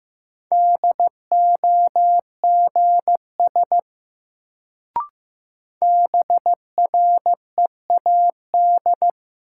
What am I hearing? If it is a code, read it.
Morse code, heard as a keyed tone, about 15 wpm, 700 Hz. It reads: DOGS BREAD
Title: Sentences from Top 1600 Words - New Word Drill Alt - 15wpm